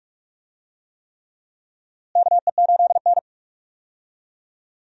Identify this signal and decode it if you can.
Morse code, heard as a keyed tone, about 46 wpm, 700 Hz. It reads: KE8N
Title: USA Call Signs - 4 Characters - Repeated 3x - 45wpm